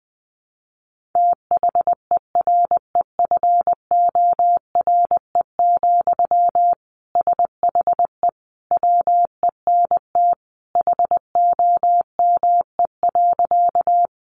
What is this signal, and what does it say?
Morse code, heard as a keyed tone, about 20 wpm, 700 Hz. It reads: THEREFORE, SHE WENT HOME.
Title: Sentences from Top 800 Words 20wpm